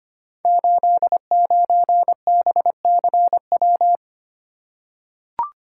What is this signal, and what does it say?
Morse code, heard as a keyed tone, about 25 wpm, 700 Hz. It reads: 89BCW
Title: Code Groups - 5 Letter:Numbers - 25wpm